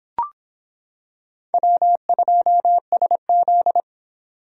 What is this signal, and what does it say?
Morse code, heard as a keyed tone, about 25 wpm, 700 Hz. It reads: W2SZ